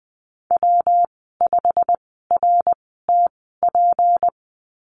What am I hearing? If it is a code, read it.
Morse code, heard as a keyed tone, about 20 wpm, spaced as for 15 wpm, 700 Hz. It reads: W5RTP
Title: 167 Lesson 27g - Number 7 - Call Signs 20wpm